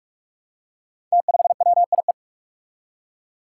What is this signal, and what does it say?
Morse code, heard as a keyed tone, about 43 wpm, 700 Hz. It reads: T5WIE
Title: Five Challenging Letter-Number Combinations 45wpm